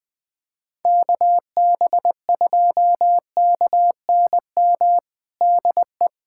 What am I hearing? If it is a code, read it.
Morse code, heard as a keyed tone, about 20 wpm, 700 Hz. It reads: KB2KNM DE
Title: USA Callsign DE Callsign - 6 Characters - 20wpm